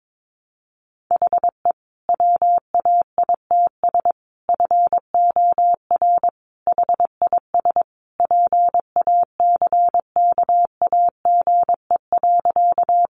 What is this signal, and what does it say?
Morse code, heard as a keyed tone, about 22 wpm, 700 Hz. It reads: HE WAITS FOR HIS PACKAGE.
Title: Additional Easy Phrases 22wpm